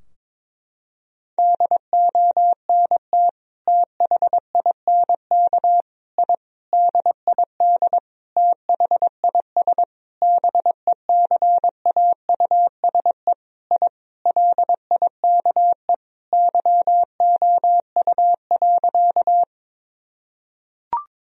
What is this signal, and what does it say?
Morse code, heard as a keyed tone, about 22 wpm, 700 Hz. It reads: DONT THINK I DID THIS BECAUSE I LIKE YOU.
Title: Sentences from Top 1000 Words - Review Part 3 - 22wpm